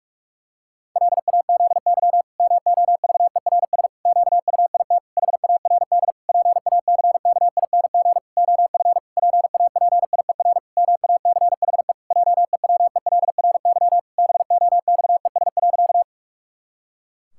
Morse code, heard as a keyed tone, about 44 wpm, 700 Hz. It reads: LAZY MOVERS QUIT HARD PACKING OF PAPIER MACHE JEWELRY BOXES.
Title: Pangram Challenge v2 - 45wpm